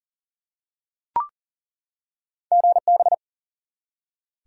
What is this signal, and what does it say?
Morse code, heard as a keyed tone, about 40 wpm, 700 Hz. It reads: GB
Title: Top 90 QSO Elements 40wpm